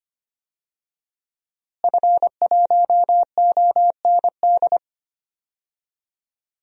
Morse code, heard as a keyed tone, about 25 wpm, 700 Hz. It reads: F1OND
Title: DXCC-100 Call Signs - 5 Characters - Repeated 2x - 25wpm